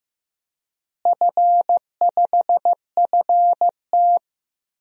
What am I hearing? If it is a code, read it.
Morse code, heard as a keyed tone, about 15 wpm, 700 Hz. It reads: F5FT
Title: DXCC-100 Call Signs - 4 Characters - Repeated 3x - 15wpm